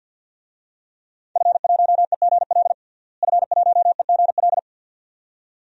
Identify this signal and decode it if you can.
Morse code, heard as a keyed tone, about 53 wpm, 700 Hz. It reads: F1EGL F1EGL
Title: DXCC-100 Call Signs - 5 Characters - Repeated 2x - 50wpm